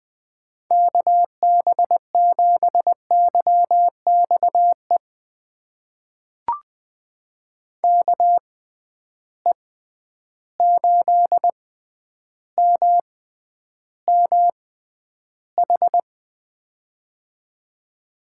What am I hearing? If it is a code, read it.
Morse code, heard as a keyed tone, about 21 wpm, spaced as for 10 wpm, 700 Hz. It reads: KB7YXE KE8MMH
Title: USA Call Signs - 6 Characters - 6x Character Spacing - 20wpm